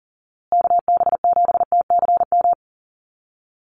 Morse code, heard as a keyed tone, about 40 wpm, 700 Hz. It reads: KB7TCM
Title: USA Call Signs Repeated 2x 40wpm